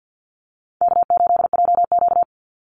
Morse code, heard as a keyed tone, about 48 wpm, 700 Hz. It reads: K8JQ
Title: Easy Call Signs Repeated 2x - 50wpm